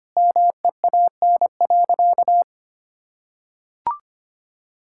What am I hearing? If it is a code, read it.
Morse code, heard as a keyed tone, about 25 wpm, 700 Hz. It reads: MEAN.
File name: practice / 25wpm / Sentences with 4-Word Phrases - 25wpm